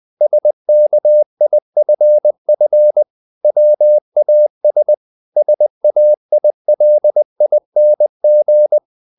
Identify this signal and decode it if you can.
Morse code, heard as a keyed tone, about 20 wpm, 600 Hz. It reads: SKIFF WAS SAILING
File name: the old man and the sea part 2 20wpm